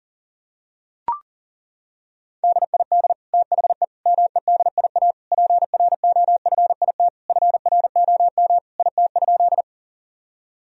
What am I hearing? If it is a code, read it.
Morse code, heard as a keyed tone, about 40 wpm, 700 Hz. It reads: DID THE MEDIA PROFIT FROM IT?
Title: Sentences from Top 1000 Words - Review Part 4 - 40wpm